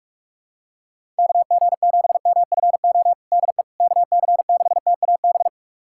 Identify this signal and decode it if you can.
Morse code, heard as a keyed tone, about 42 wpm, 700 Hz. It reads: KG7MFO DE KC6TAB
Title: USA Callsign DE Callsign 45wpm